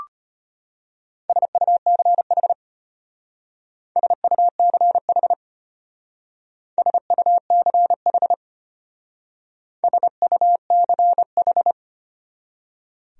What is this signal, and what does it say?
Morse code, heard as a keyed tone, about 31 wpm, 700 Hz. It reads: SUCH SUCH SUCH SUCH
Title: Speed-Racing Top 200 Words 25wpm